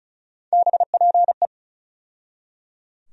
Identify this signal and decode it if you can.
Morse code, heard as a keyed tone, about 36 wpm, 700 Hz. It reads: BPE
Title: USA Call Sign Postfixes - 3 Letters 35wpm